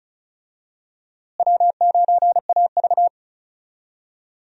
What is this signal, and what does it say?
Morse code, heard as a keyed tone, about 34 wpm, 700 Hz. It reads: W9AV
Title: USA Call Signs - 4 Characters - Repeated 2x - 35wpm